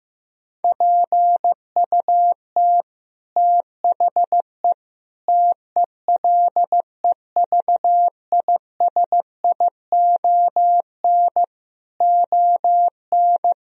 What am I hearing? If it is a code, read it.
Morse code, heard as a keyed tone, about 15 wpm, 700 Hz. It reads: PUT THE TELEVISION ON
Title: Sentences from Top 1000 Words - Review Part 4 - 15wpm